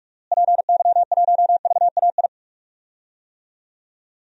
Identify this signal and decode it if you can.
Morse code, heard as a keyed tone, about 41 wpm, 700 Hz. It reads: PY1VAI
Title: DXCC-100 Call Signs - 6 Characters - Repeated 2x - 45wpm